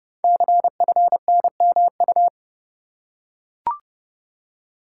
Morse code, heard as a keyed tone, about 28 wpm, 700 Hz. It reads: CFNMU
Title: Code Groups - 5 Letters - 30wpm